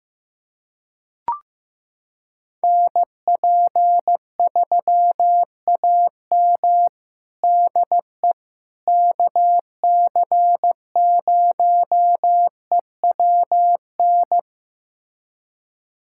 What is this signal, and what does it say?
Morse code, heard as a keyed tone, about 16 wpm, 700 Hz. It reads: NP3AM DE KC0EWN
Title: USA Callsign DE Callsign 15wpm